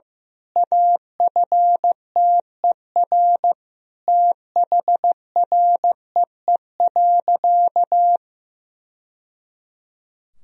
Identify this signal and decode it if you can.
Morse code, heard as a keyed tone, about 15 wpm, 700 Hz. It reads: AFTER THREE.
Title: Sentences from Top 1000 Words - Review Part 1 - 15wpm